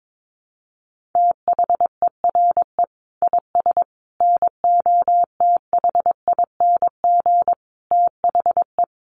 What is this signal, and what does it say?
Morse code, heard as a keyed tone, about 22 wpm, 700 Hz. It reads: THERE IS NOTHING THE
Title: Sentences from Top 700 Words 22wpm